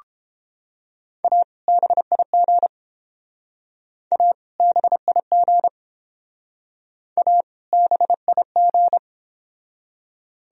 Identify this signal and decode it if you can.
Morse code, heard as a keyed tone, about 30 wpm, 700 Hz. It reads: A BIG A BIG A BIG